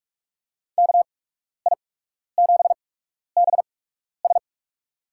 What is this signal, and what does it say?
Morse code, heard as a keyed tone, about 44 wpm, spaced as for 12 wpm, 700 Hz. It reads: KI7BS
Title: USA Call Signs - 8x Character Spacing - 45wpm